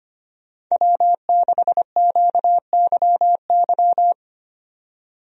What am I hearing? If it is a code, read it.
Morse code, heard as a keyed tone, about 25 wpm, 700 Hz. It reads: W6QYY
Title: USA Call Signs - 5 Characters - 25wpm